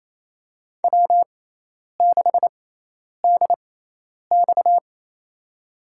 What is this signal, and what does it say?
Morse code, heard as a keyed tone, about 28 wpm, spaced as for 10 wpm, 700 Hz. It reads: W6DX
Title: USA Call Signs - 4 Characters - 6x Character Spacing - 28wpm